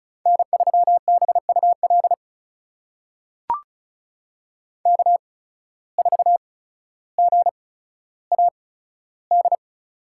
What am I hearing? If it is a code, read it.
Morse code, heard as a keyed tone, about 36 wpm, spaced as for 15 wpm, 700 Hz. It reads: N3BUL K4GAD
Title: USA Call Signs - 5 Characters - 8x Character Spacing - 35wpm